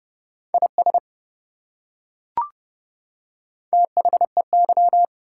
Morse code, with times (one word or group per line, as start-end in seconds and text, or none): 0.5-1.0 IS
3.7-5.0 THEY